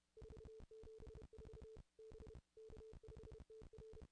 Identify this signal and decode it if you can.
Morse code, heard as a keyed tone, about 32 wpm, 440 Hz. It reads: 474 B K5TR
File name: contests / ss-cw-k5tr-op-n5rz -0100